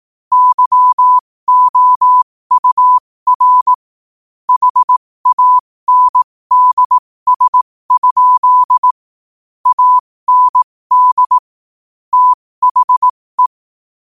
Morse code, heard as a keyed tone, about 18 wpm, spaced as for 15 wpm, 1000 Hz. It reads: YOUR HANDS? AND THE
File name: qotdfiles / 15wpm